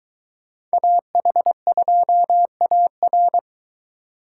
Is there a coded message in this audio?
Yes: Morse code from 0.7 to 3.4 s, heard as a keyed tone, about 21 wpm, 700 Hz. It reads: AH2AR